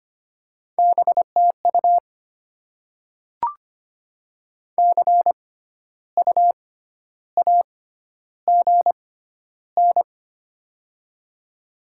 Morse code, heard as a keyed tone, about 25 wpm, spaced as for 12 wpm, 700 Hz. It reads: BTU CU AGN
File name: Top 20 QSO Elements - 6x Character Spacing - 25wpm